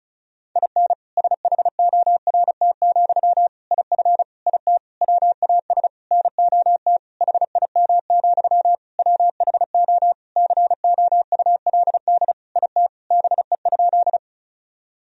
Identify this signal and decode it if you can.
Morse code, heard as a keyed tone, about 34 wpm, 700 Hz. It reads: IN SHORT, IF IT WAS NOT HIM, WHO COULD IT BE?
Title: Sentences from Top 1000 Words - Review Part 2 - 35wpm